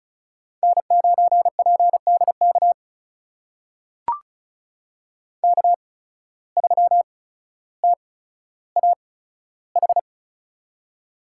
Morse code, heard as a keyed tone, about 36 wpm, spaced as for 15 wpm, 700 Hz. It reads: N9PDK K3TAH
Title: USA Call Signs - 5 Characters - 8x Character Spacing - 35wpm